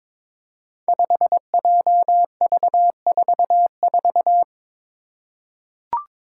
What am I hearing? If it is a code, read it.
Morse code, heard as a keyed tone, about 22 wpm, 700 Hz. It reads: 5JV44
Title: Five Challenging Letter-Number Combinations 22wpm